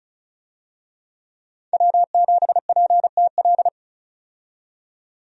Morse code, heard as a keyed tone, about 36 wpm, 700 Hz. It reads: W7PTL
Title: USA Call Signs - 5 Characters - 35wpm